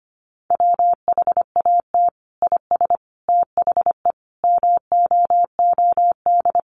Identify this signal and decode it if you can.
Morse code, heard as a keyed tone, about 25 wpm, 700 Hz. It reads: WHAT IS THE MOOD
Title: Sentences from Top 1100 Words 25wpm